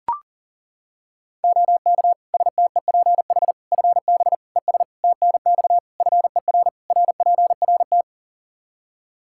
Morse code, heard as a keyed tone, about 37 wpm, 700 Hz. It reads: OK STEPH FB ES TNX FER RPRT
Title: Practice QSO - Exchange 2 - Info - 40wpm